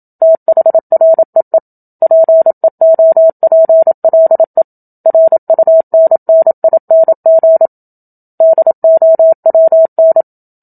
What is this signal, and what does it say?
Morse code, heard as a keyed tone, about 27 wpm, 650 Hz. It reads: THREE PEOPLE RUNNING DOWN
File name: Book Two / 01-Chapter 1 - 27 WPM